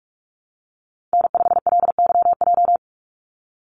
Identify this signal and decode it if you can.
Morse code, heard as a keyed tone, about 44 wpm, 700 Hz. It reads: N5LYJ
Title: Call Signs Repeated 3x - 45wpm